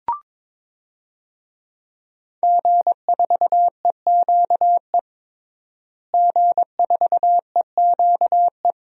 Morse code, heard as a keyed tone, about 22 wpm, 700 Hz. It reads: G4EQE G4EQE
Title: DXCC-100 Call Signs - 5 Characters - Repeated 2x - 22wpm